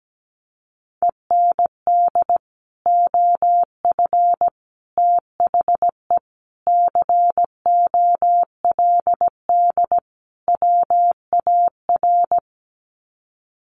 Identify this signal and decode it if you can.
Morse code, heard as a keyed tone, about 17 wpm, 700 Hz. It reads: END OF THE COLD WAR
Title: Sets of 5 Words Encore 17wpm